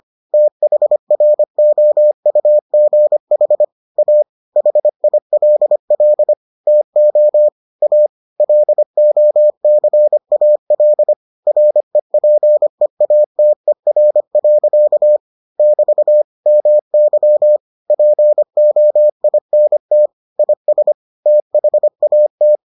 Morse code, heard as a keyed tone, about 25 wpm, 600 Hz. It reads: THROUGH A HILL TO A LOCAL REPEATER. <BT> MY POINT IS THAT